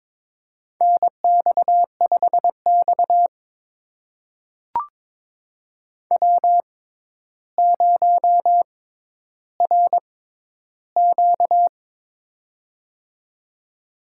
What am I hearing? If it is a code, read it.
Morse code, heard as a keyed tone, about 22 wpm, spaced as for 11 wpm, 700 Hz. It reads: NX5X W0RQ
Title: USA Call Signs - 4 Characters - 6x Character Spacing - 22wpm